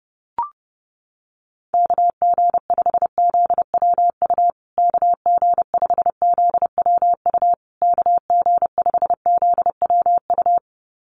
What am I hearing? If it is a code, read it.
Morse code, heard as a keyed tone, about 30 wpm, 700 Hz. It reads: KG5ZWU KG5ZWU KG5ZWU
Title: USA Call Signs Repeated 3x 30wpm